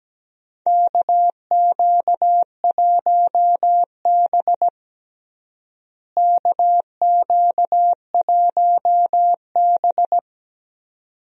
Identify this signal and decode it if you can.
Morse code, heard as a keyed tone, about 17 wpm, 700 Hz. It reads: KQ1B KQ1B
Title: USA Call Signs - 4 Characters - Repeated 2x - 17wpm